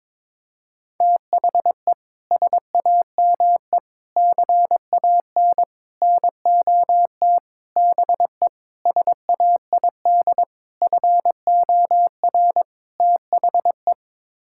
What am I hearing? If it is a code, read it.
Morse code, heard as a keyed tone, about 21 wpm, 700 Hz. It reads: THE SAME CAN NOT BE SAID FOR THE
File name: practice / 22wpm / Sets of 8 Words - 22wpm